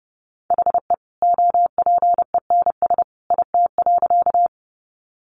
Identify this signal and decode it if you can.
Morse code, heard as a keyed tone, about 30 wpm, 700 Hz. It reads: HE OPENS IT.
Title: Additional Easy Phrases 30wpm